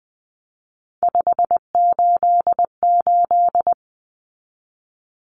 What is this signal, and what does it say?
Morse code, heard as a keyed tone, about 20 wpm, 700 Hz. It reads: 588
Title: Numbers - 3-Digit 20wpm